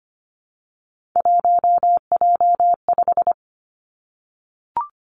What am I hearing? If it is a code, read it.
Morse code, heard as a keyed tone, about 25 wpm, 700 Hz. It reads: 1J5
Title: Three Challenging Letter-Number Combinations 25wpm